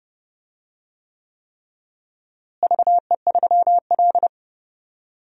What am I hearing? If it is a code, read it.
Morse code, heard as a keyed tone, about 30 wpm, 700 Hz. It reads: VE3L
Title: DXCC-100 Call Signs - 4 Characters - Repeated 3x - 30wpm